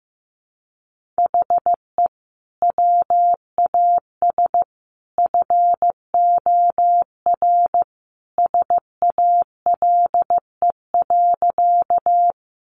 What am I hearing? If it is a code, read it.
Morse code, heard as a keyed tone, about 15 wpm, 700 Hz. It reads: HE WAS FOR SALE.
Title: Sentences from Top 1400 Words 15wpm